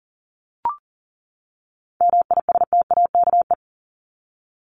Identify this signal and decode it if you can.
Morse code, heard as a keyed tone, about 40 wpm, 700 Hz. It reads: MISTAKE
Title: Top 1501-1750 Words 40wpm